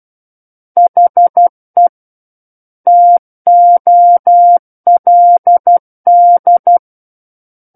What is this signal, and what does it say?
Morse code, heard as a keyed tone, about 12 wpm, 700 Hz. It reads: HE TOLD
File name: Book One / 03-Chapter 3 - 12 WPM